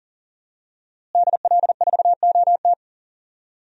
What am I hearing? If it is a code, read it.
Morse code, heard as a keyed tone, about 40 wpm, 700 Hz. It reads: DL4OT
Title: USA Call Signs - 5 Characters - Repeated 3x - 40wpm